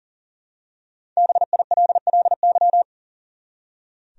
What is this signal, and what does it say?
Morse code, heard as a keyed tone, about 40 wpm, 700 Hz. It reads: BILLY
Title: USA Common Names - 40wpm